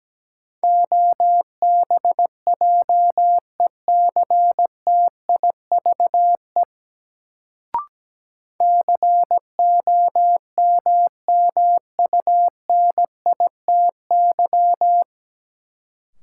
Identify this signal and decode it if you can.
Morse code, heard as a keyed tone, about 17 wpm, 700 Hz. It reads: OBJECTIVE COMMUNITY
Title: Sentences from Top 1200 Words - New Word Drill Alt - 17wpm